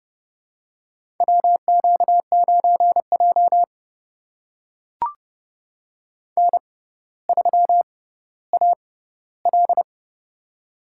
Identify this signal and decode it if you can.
Morse code, heard as a keyed tone, about 30 wpm, spaced as for 15 wpm, 700 Hz. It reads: WQ9J N3AL
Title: USA Call Signs - 6x Character Spacing - 30wpm